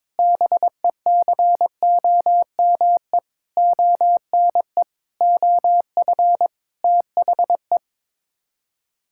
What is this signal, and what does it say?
Morse code, heard as a keyed tone, about 22 wpm, 700 Hz. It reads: BECOME ONE OF THE
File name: practice / 22wpm / Sets of 7 Words - 22wpm